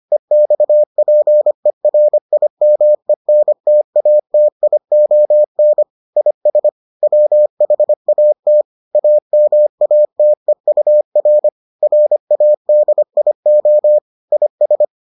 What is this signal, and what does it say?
Morse code, heard as a keyed tone, about 25 wpm, 600 Hz. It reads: EXPERIMENTATION IS WHAT AMATEUR RADIO IS